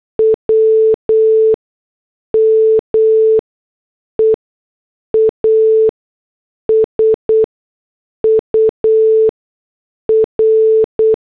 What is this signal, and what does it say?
Morse code, heard as a keyed tone, about 8 wpm, spaced as for 6 wpm, 440 Hz. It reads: W MEASUR